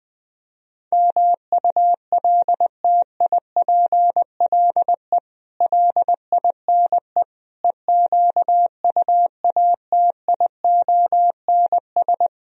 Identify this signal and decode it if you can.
Morse code, heard as a keyed tone, about 20 wpm, 700 Hz. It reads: MULTIPLE LINE EQUATIONS
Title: Speed-Racing - Sets of 5 Words 20wpm